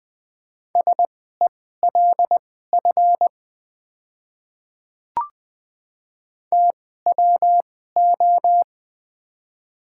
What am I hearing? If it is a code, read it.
Morse code, heard as a keyed tone, about 20 wpm, spaced as for 14 wpm, 700 Hz. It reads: SELF TWO
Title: 2x Character Spacing - Top 300 Words - 20wpm